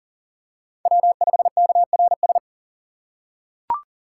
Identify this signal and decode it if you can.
Morse code, heard as a keyed tone, about 37 wpm, 700 Hz. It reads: W5KRS
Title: USA Call Signs - 5 Characters - 4x Character Spacing - 40wpm